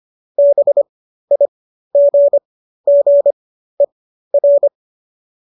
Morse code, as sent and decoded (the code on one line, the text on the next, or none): -... .. --. --. .-.
BIGGER